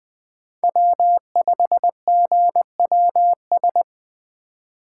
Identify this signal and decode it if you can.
Morse code, heard as a keyed tone, about 20 wpm, 700 Hz. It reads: W5GWS
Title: USA Call Signs - 5 Characters - 4x Character Spacing - 20wpm